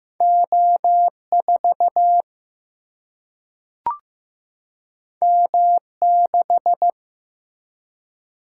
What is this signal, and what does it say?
Morse code, heard as a keyed tone, about 15 wpm, 700 Hz. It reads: O4 M6
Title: Single-Letter Single-Number 15wpm